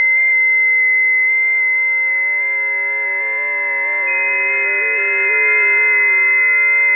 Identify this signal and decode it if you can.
FT8.
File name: tests / 20m busy / test 20